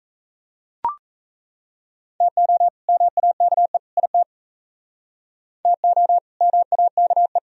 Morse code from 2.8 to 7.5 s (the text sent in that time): MAKE IT TO MAKE